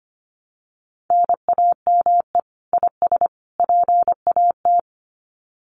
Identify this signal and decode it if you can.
Morse code, heard as a keyed tone, about 25 wpm, 700 Hz. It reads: NAME IS PAT
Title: Common QSO Phrases 25wpm